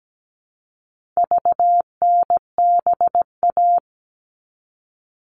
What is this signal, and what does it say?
Morse code, heard as a keyed tone, about 17 wpm, 700 Hz. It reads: VNBA
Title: All Cut Numbers - 4-Digit 17wpm